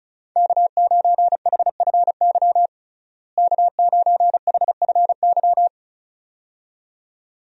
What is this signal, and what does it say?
Morse code, heard as a keyed tone, about 34 wpm, 700 Hz. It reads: K9HFY K9HFY
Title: USA Call Signs - 5 Characters - Repeated 2x - 35wpm